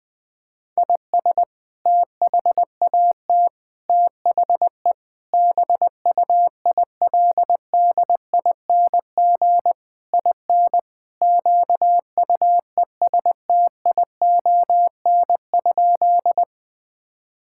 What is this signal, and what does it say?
Morse code, heard as a keyed tone, about 21 wpm, 700 Hz. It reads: IS THAT THE BUILDING IN QUESTION?
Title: Sentences from Top 1000 Words - Review Part 5 - 20wpm